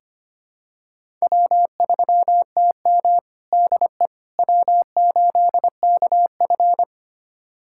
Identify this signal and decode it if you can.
Morse code, heard as a keyed tone, about 25 wpm, 700 Hz. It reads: W3TM DE W8KF
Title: USA Callsign DE Callsign - 4 Characters - 25wpm